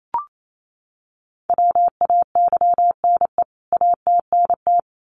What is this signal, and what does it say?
Morse code, heard as a keyed tone, about 28 wpm, 700 Hz. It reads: WAYNE ATNT
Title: CWT Exchange 28wpm